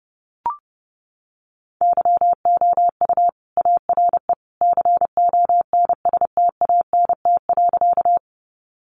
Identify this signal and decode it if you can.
Morse code, heard as a keyed tone, about 30 wpm, 700 Hz. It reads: YOU ARE CONSTANT.
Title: Sentences from Top 1500 Words 30wpm